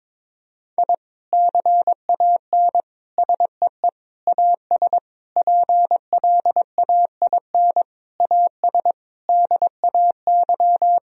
Morse code, heard as a keyed tone, about 22 wpm, 700 Hz. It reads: I CAN SEE AS PLAIN AS DAY
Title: Sentences from Top 1000 Words - Review Part 2 - 22wpm